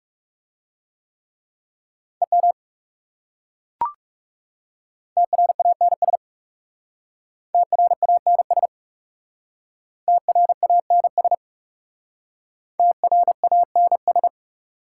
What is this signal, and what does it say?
Morse code, heard as a keyed tone, about 38 wpm, 700 Hz. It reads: EM TRANS TRANS TRANS TRANS